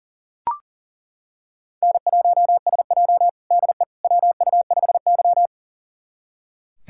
Morse code, heard as a keyed tone, about 40 wpm, 700 Hz. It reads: N1SJ DE WU5Y
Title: USA Callsign DE Callsign - 4 Characters - 40wpm